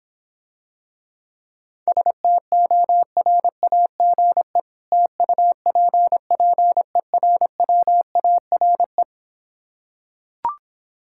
Morse code, heard as a keyed tone, about 26 wpm, 700 Hz. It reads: STORAGE TUPPERWARE